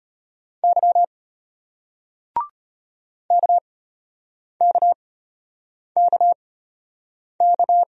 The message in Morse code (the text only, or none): Y KKKK